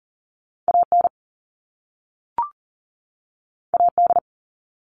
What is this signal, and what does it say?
Morse code, heard as a keyed tone, about 42 wpm, 700 Hz. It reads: AN AD
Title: All Cut Numbers - 2-Digit 40wpm